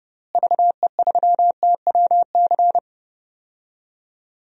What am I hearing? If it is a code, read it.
Morse code, heard as a keyed tone, about 32 wpm, 700 Hz. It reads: VE3TWC
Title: DXCC-100 Call Signs - 6 Characters - Repeated 3x - 30wpm